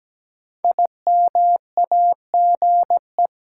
Morse code, heard as a keyed tone, about 17 wpm, 700 Hz. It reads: IMAGE
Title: Sentences from Top 700 Words - New Word Drill - 17wpm